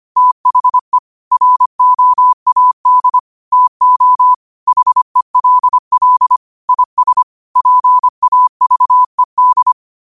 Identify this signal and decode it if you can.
Morse code, heard as a keyed tone, about 25 wpm, 1000 Hz. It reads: THE ROAD TO HELL IS PAVED
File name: qotdfiles / 25wpm